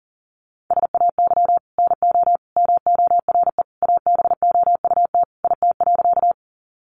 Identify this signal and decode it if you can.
Morse code, heard as a keyed tone, about 40 wpm, 700 Hz. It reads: SAY NO MORE ABOUT IT.
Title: Sentences from Top 200 Words 40wpm